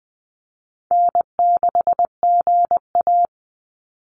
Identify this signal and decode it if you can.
Morse code, heard as a keyed tone, about 20 wpm, 700 Hz. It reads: N6GA
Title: Easy Call Signs Repeated 2x - 20wpm